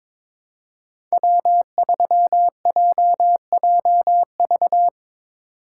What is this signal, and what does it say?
Morse code, heard as a keyed tone, about 22 wpm, 700 Hz. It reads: W3JJV
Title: USA Call Signs - 5 Characters - 2x Character Spacing - 22wpm